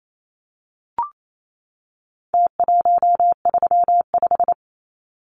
Morse code, heard as a keyed tone, about 28 wpm, 700 Hz. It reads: T135
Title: Numbers - Contest Style - 4-Digit 28wpm